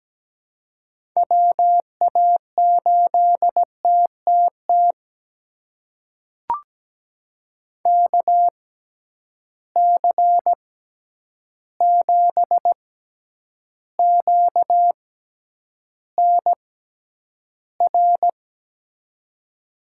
Morse code, heard as a keyed tone, about 17 wpm, spaced as for 9 wpm, 700 Hz. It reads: WA8TTT KC7QNR
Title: USA Call Signs - 6 Characters - 6x Character Spacing - 17wpm